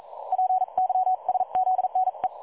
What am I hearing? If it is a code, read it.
Morse code, heard as a keyed tone, about 40 wpm, spaced as for 35 wpm, 740 Hz. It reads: PYSZNE
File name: dlazaawansowanych / nagraniazpasma / websdr recording 3565.0kHz